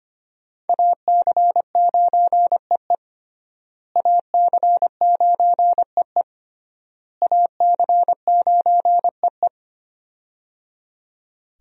Morse code, heard as a keyed tone, about 25 wpm, 700 Hz. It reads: AC9EE AC9EE AC9EE
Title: USA Call Signs - 5 Characters - Repeated 3x - 25wpm